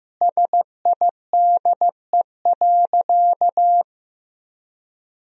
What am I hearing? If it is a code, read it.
Morse code, heard as a keyed tone, about 15 wpm, 700 Hz. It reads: SIDE.
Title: Sentences from Top 1000 Words - Review Part 1 - 15wpm